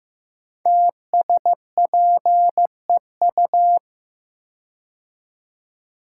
Morse code, heard as a keyed tone, about 15 wpm, 700 Hz. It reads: TSPEU
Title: Code Groups - 5 Letter:Numbers - 15wpm